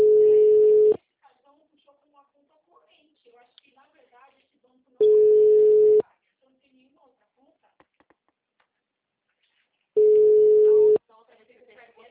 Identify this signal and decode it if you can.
Ringback tone (Europe, 425 Hz), with speech.
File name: out- -2042- - - .49487